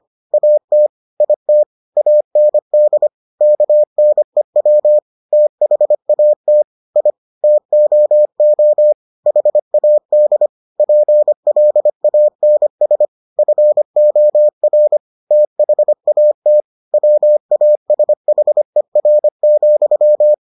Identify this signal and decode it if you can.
Morse code, heard as a keyed tone, about 25 wpm, 600 Hz. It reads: AT IT AND KNEW THAT I TOO HAD PLANS FOR THAT WASHER,